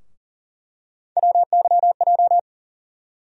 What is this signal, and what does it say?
Morse code, heard as a keyed tone, about 40 wpm, 700 Hz. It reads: WYJ
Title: USA Call Sign Postfixes - 3 Letters 40wpm